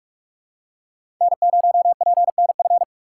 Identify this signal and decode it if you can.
Morse code, heard as a keyed tone, about 46 wpm, 690 Hz. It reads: N0PNF
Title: USA Call Signs - 5 Characters - 6x Character Spacing - 45wpm